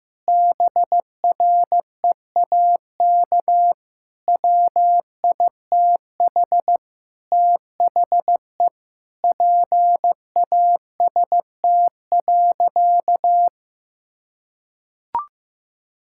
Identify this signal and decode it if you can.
Morse code, heard as a keyed tone, about 15 wpm, 700 Hz. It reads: BREAK WITH THE PAST.
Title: Sentences from Top 1000 Words - Review Part 6 - 15wpm